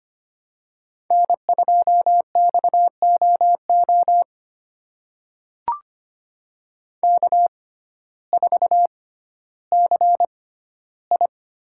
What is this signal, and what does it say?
Morse code, heard as a keyed tone, about 25 wpm, spaced as for 14 wpm, 700 Hz. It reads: N2XOO K4CI
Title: USA Call Signs - 5 Characters - 6x Character Spacing - 25wpm